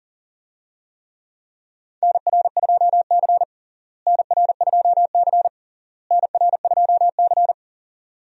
Morse code, heard as a keyed tone, about 40 wpm, 700 Hz. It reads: NR2C NR2C NR2C
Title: USA Call Signs - 4 Characters - Repeated 3x - 40wpm